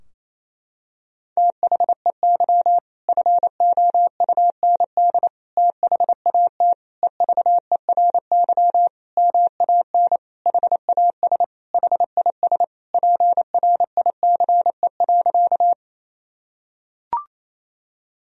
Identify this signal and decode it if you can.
Morse code, heard as a keyed tone, about 28 wpm, 700 Hz. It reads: THEY FOUND THAT EVERY MAN HAS HIS PRICE.
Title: Sentences from Top 1000 Words - Review Part 4 - 28wpm